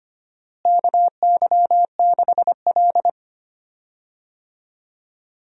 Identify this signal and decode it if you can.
Morse code, heard as a keyed tone, about 25 wpm, 700 Hz. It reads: KY6L
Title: USA Call Signs - 4 Characters - Repeated 2x - 25wpm